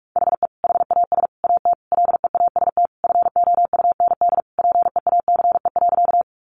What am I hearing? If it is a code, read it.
Morse code, heard as a keyed tone, about 44 wpm, 700 Hz. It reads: HE HAS AT LEAST FOUND PEACE.
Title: Sentences from Top 1500 Words 45wpm